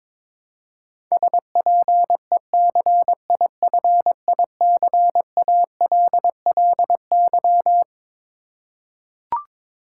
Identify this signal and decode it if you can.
Morse code, heard as a keyed tone, about 22 wpm, 700 Hz. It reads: SPECIFICALLY
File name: Twelve Letter Words - 22wpm